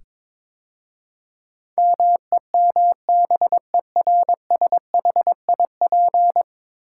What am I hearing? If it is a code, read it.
Morse code, heard as a keyed tone, about 22 wpm, 700 Hz. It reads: MEMBERSHIP
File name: practice / 22wpm / 10 Letter Words - First 3 Letters Given - 22wpm